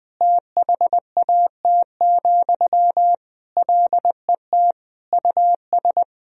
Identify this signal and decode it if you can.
Morse code, heard as a keyed tone, about 20 wpm, 700 Hz. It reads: THAT, LET US
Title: Sentences from Top 1000 Words - Review Part 4 - 20wpm